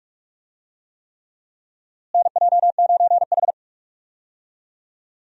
Morse code, heard as a keyed tone, about 46 wpm, 700 Hz. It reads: NJ9H
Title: USA Call Signs - 4 Characters - Repeated 3x - 45wpm